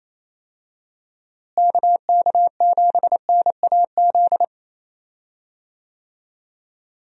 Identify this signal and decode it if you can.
Morse code, heard as a keyed tone, about 28 wpm, 700 Hz. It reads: KK7NAZ
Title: USA Call Signs - 6 Characters - Repeated 3x - 28wpm